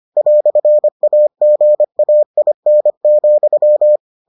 Morse code, heard as a keyed tone, about 25 wpm, 600 Hz. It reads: "AGAIN,